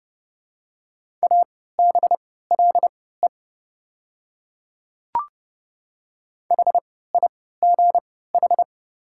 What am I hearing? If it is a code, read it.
Morse code, heard as a keyed tone, about 30 wpm, spaced as for 17 wpm, 690 Hz. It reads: ABLE HIGH
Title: 3x Character Spacing - Top 500 Words - 30wpm